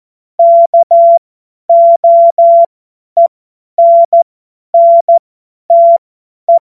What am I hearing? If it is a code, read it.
Morse code, heard as a keyed tone, about 14 wpm, spaced as for 10 wpm, 680 Hz. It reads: KOENNTE
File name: Morsekurs / Witze / Witz 11. Mai 2021, 50 F70